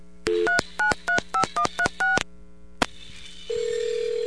DTMF, heard as telephone touch tones, tones of about 100 ms. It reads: 3935136